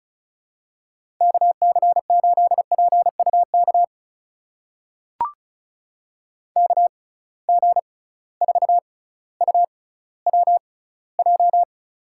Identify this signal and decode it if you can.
Morse code, heard as a keyed tone, about 34 wpm, spaced as for 18 wpm, 700 Hz. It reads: KC8PUK KG4UWJ
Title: USA Call Signs - 6 Characters - 6x Character Spacing - 35wpm